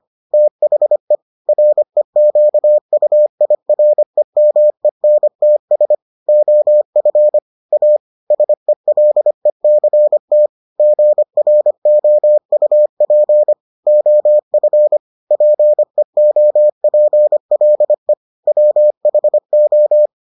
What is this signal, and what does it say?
Morse code, heard as a keyed tone, about 25 wpm, 600 Hz. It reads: THE REQUIREMENTS OF A SELECT GROUP OF PEOPLE WHO